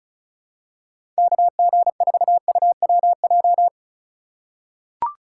Morse code, heard as a keyed tone, about 34 wpm, 700 Hz. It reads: KG4UWJ
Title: USA Call Signs - 6 Characters - 3x Character Spacing - 35wpm